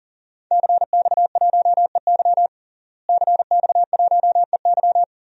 Morse code, heard as a keyed tone, about 40 wpm, 700 Hz. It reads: CX1EY CX1EY
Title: DXCC-100 Call Signs - 5 Characters - Repeated 3x - 40wpm